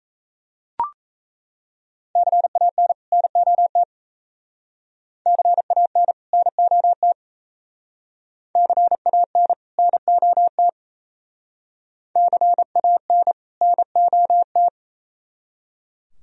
Morse code, heard as a keyed tone, about 34 wpm, 700 Hz. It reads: CAN NOT CAN NOT CAN NOT CAN NOT
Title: Speed-Racing - Sets of 2 Words 28wpm